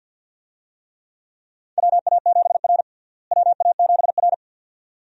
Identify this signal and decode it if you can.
Morse code, heard as a keyed tone, about 52 wpm, 700 Hz. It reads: WA7R WA7R
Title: USA Call Signs - 4 Characters - Repeated 2x - 50wpm